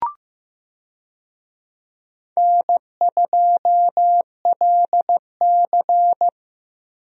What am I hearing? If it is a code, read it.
Morse code, heard as a keyed tone, about 15 wpm, 700 Hz. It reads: N2LC
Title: USA Call Signs - 4 Characters - Repeated 3x - 15wpm